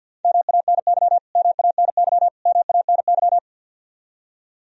Morse code, heard as a keyed tone, about 50 wpm, 700 Hz. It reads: MANY MANY MANY
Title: Top 100 Words Repeated 50wpm